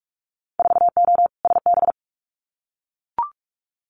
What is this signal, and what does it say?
Morse code, heard as a keyed tone, about 44 wpm, 700 Hz. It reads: 4O SB
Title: Winter Field Day Exchanges 45wpm